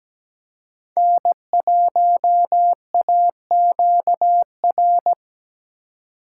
Morse code, heard as a keyed tone, about 17 wpm, 700 Hz. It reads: N1AQR
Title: USA Call Signs - 5 Characters - 2x Character Spacing - 17wpm